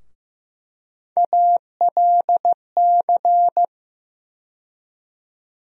Morse code, heard as a keyed tone, about 15 wpm, 700 Hz. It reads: ALC
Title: USA Call Sign Postfixes - 3 Letters 15wpm